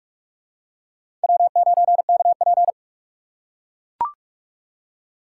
Morse code, heard as a keyed tone, about 44 wpm, 700 Hz. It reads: W9KP
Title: USA Call Signs - 4 Characters - 6x Character Spacing - 45wpm